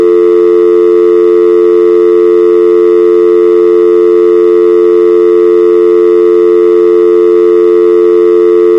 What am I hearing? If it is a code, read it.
Dial tone (North America), 350 + 440 Hz.
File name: Dial Tone - 60 seconds - North American